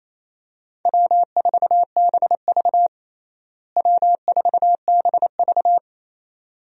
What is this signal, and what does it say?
Morse code, heard as a keyed tone, about 27 wpm, 700 Hz. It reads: W4BV W4BV
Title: USA Call Signs - 4 Characters - Repeated 2x - 28wpm